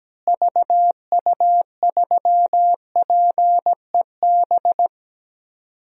Morse code, heard as a keyed tone, about 17 wpm, 700 Hz. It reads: VU3PEB
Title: DXCC-100 Call Signs - 6 Characters - Repeated 3x - 17wpm